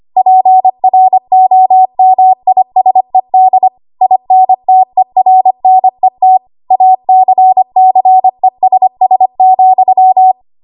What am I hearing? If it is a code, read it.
Morse code, heard as a keyed tone, about 25 wpm, 750 Hz. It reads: PROMISED INTERNET ACCESS,
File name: Archive / 25 WPM / 25WPM